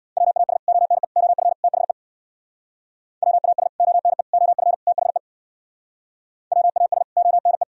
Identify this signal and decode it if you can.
Morse code, heard as a keyed tone, about 65 wpm, 690 Hz. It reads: WAS ONE OF THE WAS ONE OF THE WAS ONE